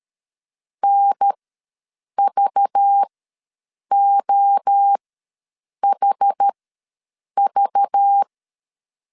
Morse code, heard as a keyed tone, about 13 wpm, spaced as for 7 wpm, 800 Hz. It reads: NVOHV
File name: LEZIONI / 5 LBVC / 5 LBV ws 12wpm 3015